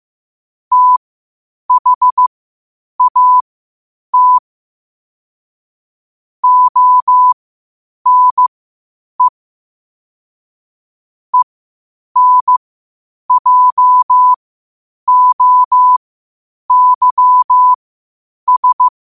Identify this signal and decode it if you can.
Morse code, heard as a keyed tone, about 15 wpm, spaced as for 8 wpm, 1000 Hz. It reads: THAT ONE ENJOYS
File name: qotdfiles / 8wpm